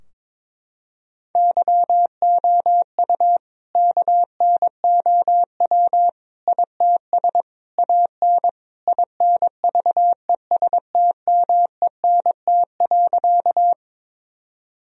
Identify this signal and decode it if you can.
Morse code, heard as a keyed tone, about 22 wpm, 700 Hz. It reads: YOU KNOW ITS AN INVESTMENT.
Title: Sentences from Top 1000 Words - Review Part 4 - 22wpm